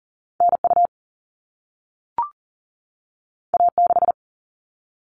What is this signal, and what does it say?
Morse code, heard as a keyed tone, about 40 wpm, 700 Hz. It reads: NU A6
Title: All Cut Numbers - 2-Digit 40wpm